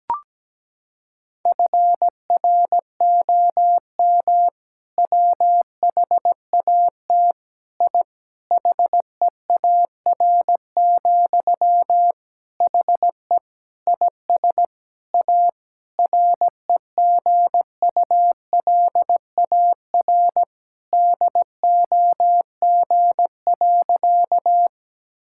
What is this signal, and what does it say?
Morse code, heard as a keyed tone, about 17 wpm, 700 Hz. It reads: FROM WHAT I HEAR, HE IS A REGULAR DOG.
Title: Sentences from Top 1000 Words - Review Part 6 - 17wpm